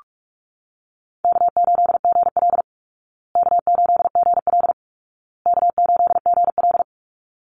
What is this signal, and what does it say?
Morse code, heard as a keyed tone, about 44 wpm, 700 Hz. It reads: K8GL K8GL K8GL
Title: Easy Call Signs Repeated 3x - 45wpm